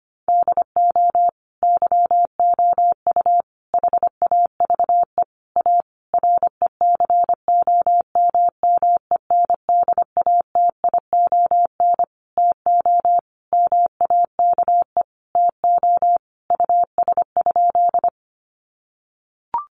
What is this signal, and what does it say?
Morse code, heard as a keyed tone, about 25 wpm, 700 Hz. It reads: DO YOU HAVE A RECOMMENDATION TO MAKE TO US?
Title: Sentences from Top 1400 Words 25wpm